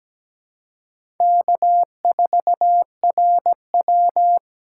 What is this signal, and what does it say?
Morse code, heard as a keyed tone, about 17 wpm, 700 Hz. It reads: K4RW
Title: USA Call Signs - 4x Character Spacing - 17wpm